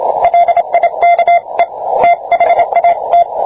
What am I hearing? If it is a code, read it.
Morse code, heard as a keyed tone, about 27 wpm, 690 Hz, with radio noise. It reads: LIKE THAT